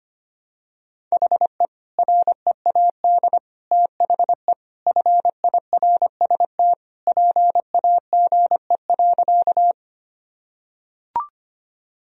Morse code, heard as a keyed tone, about 25 wpm, 700 Hz. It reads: HE READ THE FIRST PAGE.
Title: Sentences from Top 1000 Words - Review Part 1 - 25wpm